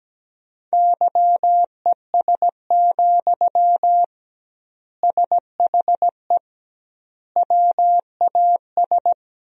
Morse code, heard as a keyed tone, about 17 wpm, 700 Hz. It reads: YES, SHE WAS
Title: 2x Word-Spacing - Sentences from Top 500 Words - 17wpm